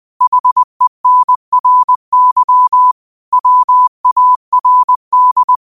Morse code, heard as a keyed tone, about 20 wpm, 1000 Hz. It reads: HENRY WARD